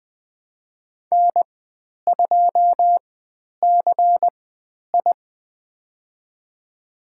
Morse code, heard as a keyed tone, about 20 wpm, spaced as for 10 wpm, 700 Hz. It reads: N2CI
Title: 109 Lesson 18g - Letter m - Call Signs 20wpm